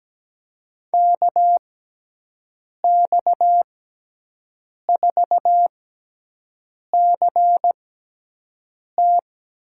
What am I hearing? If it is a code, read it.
Morse code, heard as a keyed tone, about 17 wpm, spaced as for 6 wpm, 700 Hz. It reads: KX4CT
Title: USA Call Signs - 6x Character Spacing - 17wpm